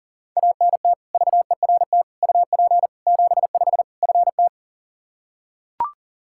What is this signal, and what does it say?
Morse code, heard as a keyed tone, about 37 wpm, 700 Hz. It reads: ANT VERT UP 75 FT
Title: Practice QSO - Exchange 2 - Info - 40wpm